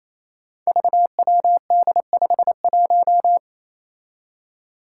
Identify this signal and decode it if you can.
Morse code, heard as a keyed tone, about 28 wpm, 700 Hz. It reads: VWD51
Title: Five Challenging Letter-Number Combinations 28wpm